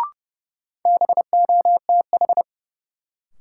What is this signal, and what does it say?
Morse code, heard as a keyed tone, about 30 wpm, 700 Hz. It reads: BOTH
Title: Sentences from Top 300 Words - New Word Drill - 30wpm